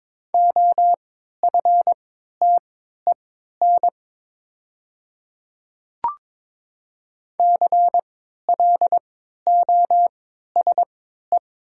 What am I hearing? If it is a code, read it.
Morse code, heard as a keyed tone, about 22 wpm, spaced as for 12 wpm, 700 Hz. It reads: OFTEN CLOSE
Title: 3x Character Spacing - Top 300 Words - 22wpm